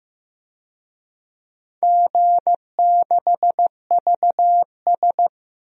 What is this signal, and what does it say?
Morse code, heard as a keyed tone, about 15 wpm, 700 Hz. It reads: G6VS
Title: DXCC-100 Call Signs - 4 Characters - 15wpm